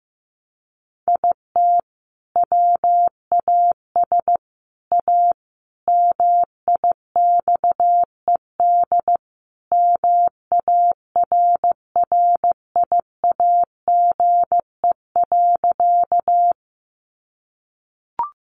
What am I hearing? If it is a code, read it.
Morse code, heard as a keyed tone, about 14 wpm, 700 Hz. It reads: IT WAS A MIXED MARRIAGE.
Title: Sentences from Top 1500 Words 15wpm